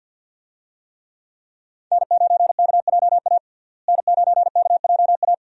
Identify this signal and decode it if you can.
Morse code, heard as a keyed tone, about 46 wpm, 700 Hz. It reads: N9KJA N9KJA
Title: USA Call Signs - 5 Characters - Repeated 2x - 50wpm